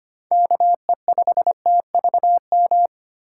Morse code, heard as a keyed tone, about 25 wpm, 700 Hz. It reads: KE5TVM
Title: USA Call Signs - 6 Characters - 3x Character Spacing - 25wpm